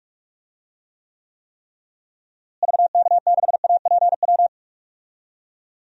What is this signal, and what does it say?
Morse code, heard as a keyed tone, about 45 wpm, 700 Hz. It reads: VK6APW
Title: DXCC-100 Call Signs - 6 Characters - Repeated 3x - 45wpm